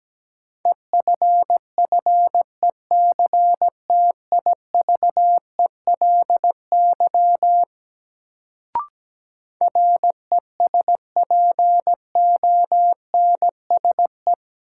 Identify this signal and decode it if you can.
Morse code, heard as a keyed tone, about 17 wpm, 700 Hz. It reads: EFFECTIVELY RESPONSE
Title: Sentences from Top 1400 Words - New Word Drill Alt - 17wpm